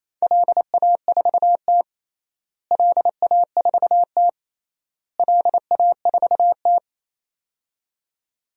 Morse code, heard as a keyed tone, about 27 wpm, 700 Hz. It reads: LA4T LA4T LA4T
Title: DXCC-100 Call Signs - 4 Characters - Repeated 3x - 28wpm